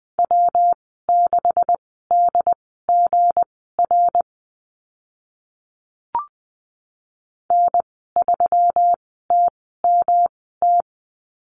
Morse code, heard as a keyed tone, about 20 wpm, spaced as for 14 wpm, 700 Hz. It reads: W6DGR N3TMT
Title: 186 Lesson 30g - Letter b - Call Signs 20wpm